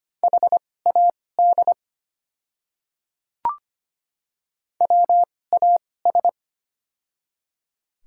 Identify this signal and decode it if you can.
Morse code, heard as a keyed tone, about 25 wpm, spaced as for 18 wpm, 700 Hz. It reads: HAD WAS
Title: 2x Character Spacing - Top 200 Words - 25wpm